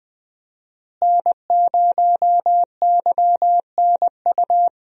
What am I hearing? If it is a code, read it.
Morse code, heard as a keyed tone, about 20 wpm, 700 Hz. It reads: N0YNU
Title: USA Call Signs - 5 Characters - Repeated 2x - 20wpm